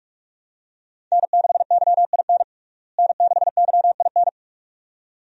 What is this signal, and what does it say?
Morse code, heard as a keyed tone, about 46 wpm, 700 Hz. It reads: N6YIN N6YIN
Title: USA Call Signs - 5 Characters - Repeated 2x - 45wpm